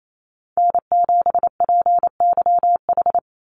Morse code, heard as a keyed tone, about 28 wpm, 700 Hz. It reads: N7PYH
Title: USA Call Signs 28wpm